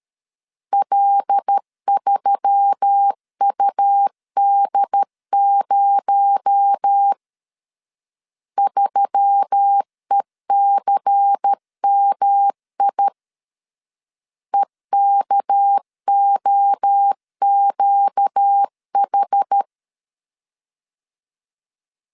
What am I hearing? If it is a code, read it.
Morse code, heard as a keyed tone, about 13 wpm, 800 Hz. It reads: L3UD0 3ECMI EKOQH
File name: LEZIONI / 6 PKX0 / 6 PKX0 12wpm 2231